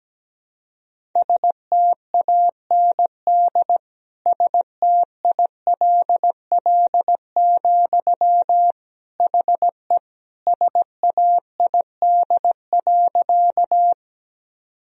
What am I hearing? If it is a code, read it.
Morse code, heard as a keyed tone, about 17 wpm, 700 Hz. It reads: STAND STILL, HE SAID.
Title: Sentences from Top 1000 Words - Review Part 1 - 17wpm